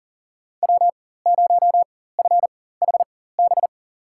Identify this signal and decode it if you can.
Morse code, heard as a keyed tone, about 40 wpm, spaced as for 19 wpm, 700 Hz. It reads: W0FHB
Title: USA Call Signs - 5 Characters - 4x Character Spacing - 40wpm